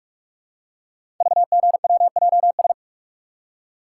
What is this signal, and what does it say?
Morse code, heard as a keyed tone, about 44 wpm, 700 Hz. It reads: VGWJS
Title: Code Groups - 5 Letters - 45wpm